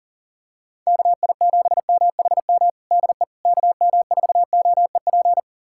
Morse code, heard as a keyed tone, about 40 wpm, 690 Hz. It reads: KI7MHM DE KM4OEP
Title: USA Callsign DE Callsign - 6 Characters - 40wpm